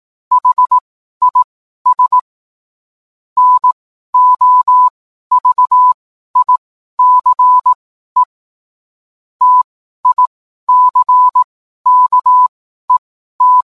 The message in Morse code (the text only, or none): HIS NOVICE TICKET